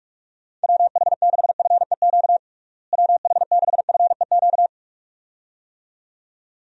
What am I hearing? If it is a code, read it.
Morse code, heard as a keyed tone, about 46 wpm, 700 Hz. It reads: WH6FEQ WH6FEQ
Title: USA Call Signs - 6 Characters - Repeated 2x - 45wpm